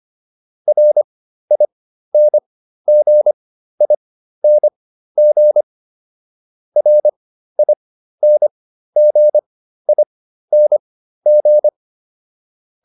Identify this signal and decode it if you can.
Morse code, heard as a keyed tone, about 25 wpm, spaced as for 13 wpm, 600 Hz. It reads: RINGING RINGING